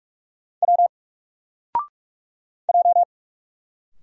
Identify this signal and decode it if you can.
Morse code, heard as a keyed tone, about 46 wpm, 700 Hz. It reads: W J
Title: Single Letter-Number 45wpm